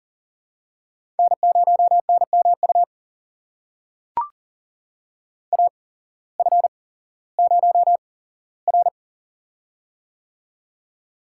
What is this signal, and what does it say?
Morse code, heard as a keyed tone, about 40 wpm, spaced as for 19 wpm, 700 Hz. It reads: N0NMU AF0R
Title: USA Call Signs - 8x Character Spacing - 40wpm